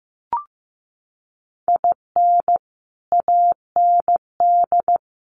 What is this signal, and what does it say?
Morse code, heard as a keyed tone, about 15 wpm, 700 Hz. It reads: IN AND
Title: Sets of 4 Words 15wpm